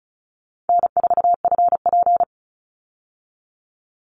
Morse code, heard as a keyed tone, about 34 wpm, 700 Hz. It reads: N4FP
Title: Easy Call Signs Repeated 2x - 35wpm